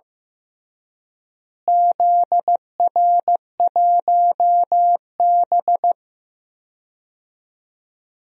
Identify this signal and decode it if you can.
Morse code, heard as a keyed tone, about 15 wpm, 700 Hz. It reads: ZR1B
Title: DXCC-100 Call Signs - 4 Characters - Repeated 2x - 15wpm